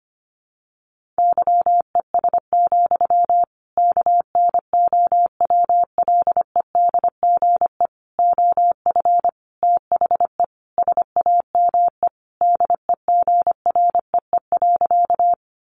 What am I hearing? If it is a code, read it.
Morse code, heard as a keyed tone, about 25 wpm, 700 Hz. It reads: YES, KNOWLEDGE OF THE SAME DEGREE.
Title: Sentences from Top 1500 Words 25wpm